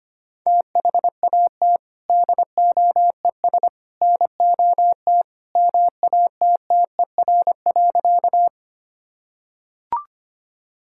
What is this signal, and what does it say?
Morse code, heard as a keyed tone, about 25 wpm, 700 Hz. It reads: THAT DOES NOT MATTER.
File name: practice / 25wpm / Sentences with 3-Word Phrases - 25wpm